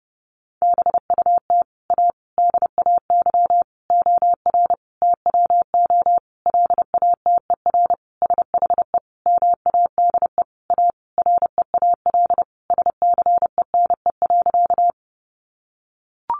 Morse code, heard as a keyed tone, about 30 wpm, 700 Hz. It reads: BUT A DAY OR TWO LATER SHE MADE A REAL SCENE.
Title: Sentences from Top 1100 Words 30wpm